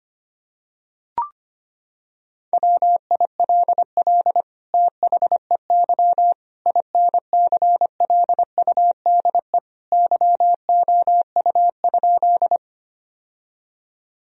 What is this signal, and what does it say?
Morse code, heard as a keyed tone, about 25 wpm, 700 Hz. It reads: WILL THEY INCLUDE YOU?
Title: Sentences from Top 1000 Words - Review Part 3 - 25wpm